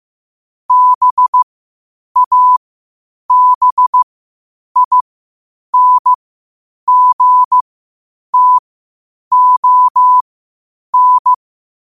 Morse code, heard as a keyed tone, about 15 wpm, spaced as for 8 wpm, 1000 Hz. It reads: BABINGTON
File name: qotdfiles / 8wpm